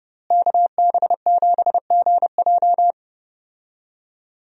Morse code, heard as a keyed tone, about 32 wpm, 700 Hz. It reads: KB7GJ
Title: USA Call Signs - 5 Characters - Repeated 3x - 30wpm